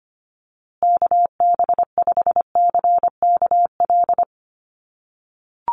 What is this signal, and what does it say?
Morse code, heard as a keyed tone, about 25 wpm, 700 Hz. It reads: KB5CKL
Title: USA Call Signs Repeated 3x 25wpm